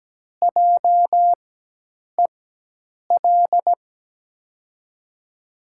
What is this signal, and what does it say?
Morse code, heard as a keyed tone, about 17 wpm, spaced as for 8 wpm, 700 Hz. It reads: JEL
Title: USA Call Signs - 6 Characters - 4x Character Spacing - 17wpm